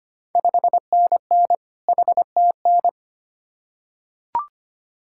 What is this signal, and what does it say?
Morse code, heard as a keyed tone, about 25 wpm, 700 Hz. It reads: 5NN HTN
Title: Ontario QSO Party Exchange 25wpm